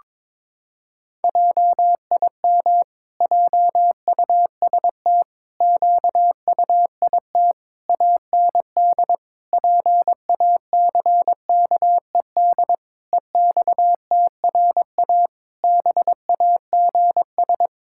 Morse code, heard as a keyed tone, about 22 wpm, 700 Hz. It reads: JIM JUST QUIT AND PACKED EXTRA BAGS
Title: Pangram Challenge v2 - 22wpm